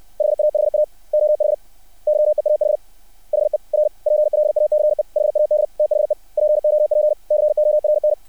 Morse code, heard as a keyed tone, about 60 wpm, 600 Hz. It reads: HAHA OH YEAH HER PLACE HAD THE POPCORN
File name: archived / CRSnet